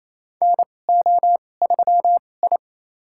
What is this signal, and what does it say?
Morse code, heard as a keyed tone, about 28 wpm, spaced as for 20 wpm, 700 Hz. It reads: NO3I
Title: USA Call Signs - 4 Characters - 2x Character Spacing - 28wpm